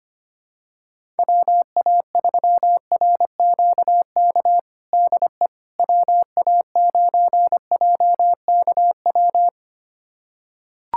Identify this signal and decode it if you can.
Morse code, heard as a keyed tone, about 25 wpm, 700 Hz. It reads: WA3RQK DE WA9JKW
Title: USA Callsign DE Callsign - 6 Characters - 25wpm